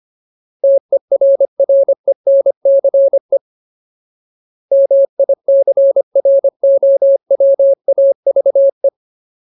Morse code, heard as a keyed tone, about 25 wpm, 550 Hz. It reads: TERRENCE MICROWAVE
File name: Ultimate Word List / 25WPM Ultimate